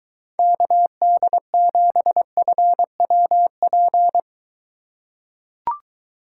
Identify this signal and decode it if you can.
Morse code, heard as a keyed tone, about 22 wpm, 700 Hz. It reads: KD7FWP